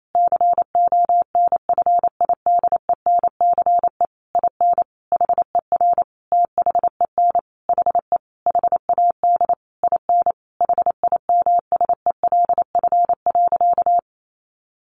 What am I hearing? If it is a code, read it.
Morse code, heard as a keyed tone, about 28 wpm, 700 Hz. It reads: CONFIDENCE IN HER THEN HE HAD IN HIMSELF.
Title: Sentences from Top 1500 Words 28wpm